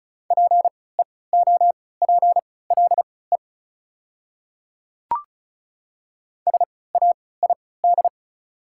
Morse code, heard as a keyed tone, about 34 wpm, spaced as for 20 wpm, 690 Hz. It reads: PEOPLE SAID
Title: 3x Character Spacing - Top 100 Words - 35wpm